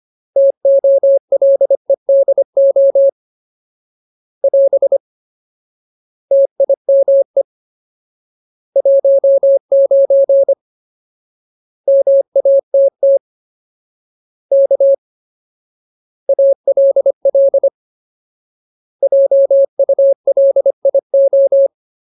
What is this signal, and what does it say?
Morse code, heard as a keyed tone, about 25 wpm, 550 Hz. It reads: TOLEDO <AS> TIME 19 MATT K ALL JULIO
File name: Ultimate Word List / 25WPM Ultimate